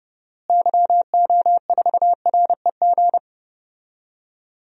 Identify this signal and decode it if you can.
Morse code, heard as a keyed tone, about 30 wpm, 700 Hz. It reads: YO4REG
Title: DXCC-100 Call Signs - 6 Characters - Repeated 3x - 30wpm